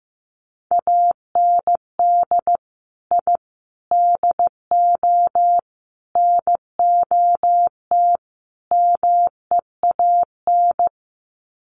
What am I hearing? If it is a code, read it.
Morse code, heard as a keyed tone, about 15 wpm, 700 Hz. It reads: AND I DO NOT MEAN
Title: Sets of 5 Words Encore2 15wpm